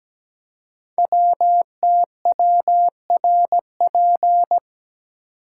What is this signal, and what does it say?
Morse code, heard as a keyed tone, about 17 wpm, 700 Hz. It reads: WTWRP
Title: Code Groups - 5 Letters - 17wpm